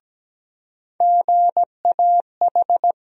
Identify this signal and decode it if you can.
Morse code, heard as a keyed tone, about 17 wpm, 700 Hz. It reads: GAH
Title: USA Call Sign Postfixes - 3 Letters 17wpm